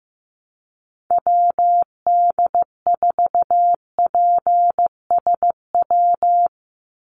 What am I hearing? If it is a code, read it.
Morse code, heard as a keyed tone, about 15 wpm, 700 Hz. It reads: WD4PSW
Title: USA Call Signs 15wpm